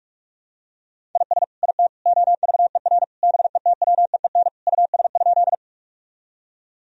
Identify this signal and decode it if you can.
Morse code, heard as a keyed tone, about 42 wpm, 700 Hz. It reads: IS IT OVER BETWEEN US?
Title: Sentences from Top 1000 Words - Review Part 1 - 45wpm